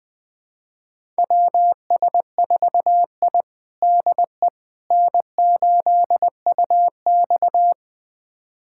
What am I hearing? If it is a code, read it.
Morse code, heard as a keyed tone, about 20 wpm, 700 Hz. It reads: WS4I DE N8UX
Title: USA Callsign DE Callsign - 4 Characters - 20wpm